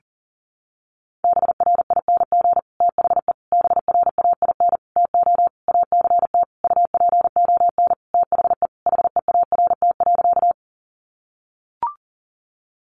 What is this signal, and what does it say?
Morse code, heard as a keyed tone, about 40 wpm, 700 Hz. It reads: BRING THE BRAIN TO ACT UPON THE HEART.
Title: Sentences from Top 1200 Words 40wpm